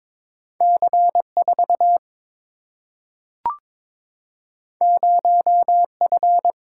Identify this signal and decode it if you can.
Morse code, heard as a keyed tone, about 22 wpm, 700 Hz. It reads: C 4 0 F
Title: Single-Letter Single-Number 22wpm